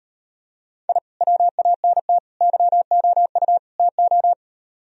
Morse code, heard as a keyed tone, about 39 wpm, 690 Hz. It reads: I WANT YOU TO